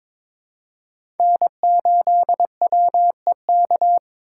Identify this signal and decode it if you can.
Morse code, heard as a keyed tone, about 22 wpm, 700 Hz. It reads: N8WEK
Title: USA Call Signs - 5 Characters - 3x Character Spacing - 22wpm